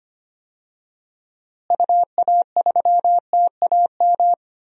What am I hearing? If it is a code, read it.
Morse code, heard as a keyed tone, about 25 wpm, 700 Hz. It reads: UA3TAM
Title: DXCC-100 Call Signs - 6 Characters - Repeated 2x - 25wpm